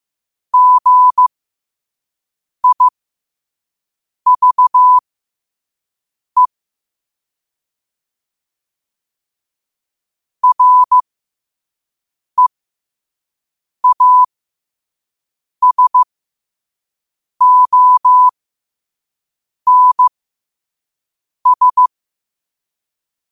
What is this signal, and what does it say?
Morse code, heard as a keyed tone, about 15 wpm, spaced as for 5 wpm, 1000 Hz. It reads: GIVE REASONS
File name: qotdfiles / 5wpm